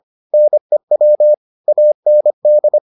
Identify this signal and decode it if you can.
Morse code, heard as a keyed tone, about 25 wpm, 600 Hz. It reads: NEW AND